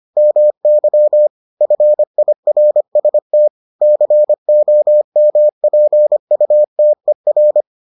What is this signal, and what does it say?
Morse code, heard as a keyed tone, about 25 wpm, 600 Hz. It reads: MY FIRST COMPUTER